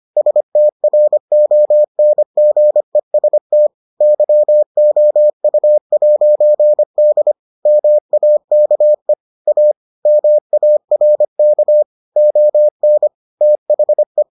Morse code, heard as a keyed tone, about 25 wpm, 600 Hz. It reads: STRONGEST YOU'D MAKE A MARK ON THE